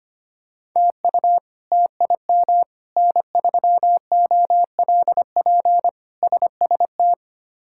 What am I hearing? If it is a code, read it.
Morse code, heard as a keyed tone, about 25 wpm, 700 Hz. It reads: TU TIM N3OLP SST
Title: K1USN SST QSO Practice - 25wpm